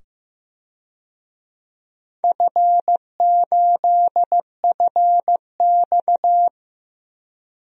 Morse code, heard as a keyed tone, about 15 wpm, 700 Hz. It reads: F8FX
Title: DXCC-100 Call Signs - 4 Characters - 15wpm